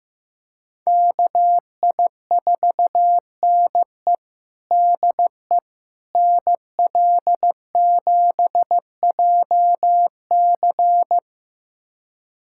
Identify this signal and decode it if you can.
Morse code, heard as a keyed tone, about 15 wpm, 700 Hz. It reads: KI4NE DE NL7JC
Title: USA Callsign DE Callsign 15wpm